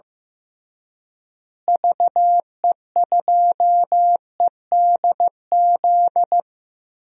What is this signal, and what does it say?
Morse code, heard as a keyed tone, about 15 wpm, 700 Hz. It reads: VE2EDZ
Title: DXCC-100 Call Signs - 6 Characters - Repeated 3x - 15wpm